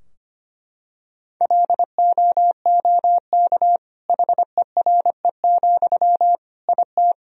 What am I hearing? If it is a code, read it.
Morse code, heard as a keyed tone, about 25 wpm, 700 Hz. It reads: LOOK HERE, IT
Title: Sentences from Top 1000 Words - Review Part 2 - 25wpm